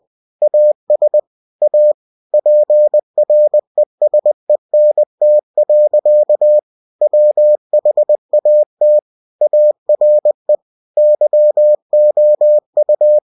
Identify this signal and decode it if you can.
Morse code, heard as a keyed tone, about 20 wpm, 600 Hz. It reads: AS A PRESENT. WHAT ARE YOU
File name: cw-books / winnie-the-pooh / winnie-20wpm-page-0049